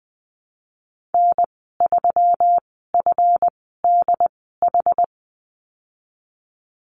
Morse code, heard as a keyed tone, about 20 wpm, spaced as for 15 wpm, 700 Hz. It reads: N3FDH
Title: 161 Lesson 26g - Letter g - Call Signs 20wpm